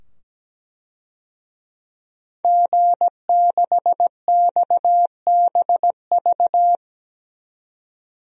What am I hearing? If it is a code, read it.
Morse code, heard as a keyed tone, about 17 wpm, 700 Hz. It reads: G6XBV
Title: DXCC-100 Call Signs - 5 Characters - 17wpm